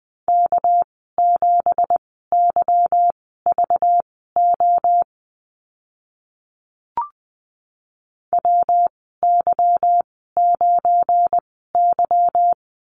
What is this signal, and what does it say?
Morse code, heard as a keyed tone, about 20 wpm, spaced as for 14 wpm, 700 Hz. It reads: K7YVO WY9Y
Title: 219 Lesson 35g - Number 0 - Call Signs 20wpm